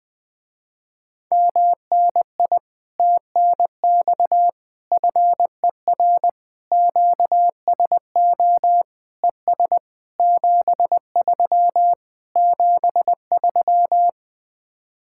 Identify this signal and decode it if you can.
Morse code, heard as a keyed tone, about 20 wpm, 700 Hz. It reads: MNI TNX FER QSO ES 73 73
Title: Speed-Racing Common QSO Phrases - 17wpm